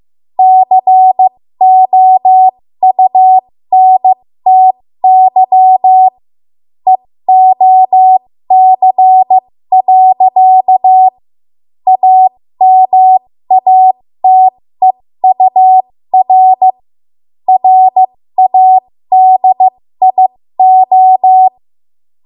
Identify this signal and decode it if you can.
Morse code, heard as a keyed tone, about 15 wpm, spaced as for 13 wpm, 750 Hz. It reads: COUNTY EOC. AMATEUR RADIO
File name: Morse / 13WPM